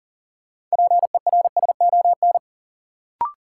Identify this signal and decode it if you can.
Morse code, heard as a keyed tone, about 40 wpm, 700 Hz. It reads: PERSON
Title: Sentences from Top 400 Words - New Word Drill - 40wpm